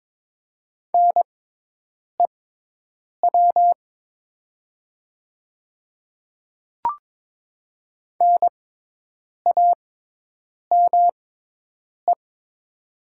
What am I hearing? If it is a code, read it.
Morse code, heard as a keyed tone, about 22 wpm, spaced as for 8 wpm, 700 Hz. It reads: NEW NAME
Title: 6x Character Spacing - Top 200 Words - 22wpm